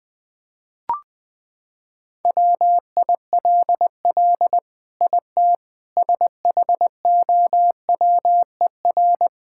Morse code, heard as a keyed tone, about 21 wpm, 700 Hz. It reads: WILL IT SHOWER
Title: Sentences from Top 1600 Words 20wpm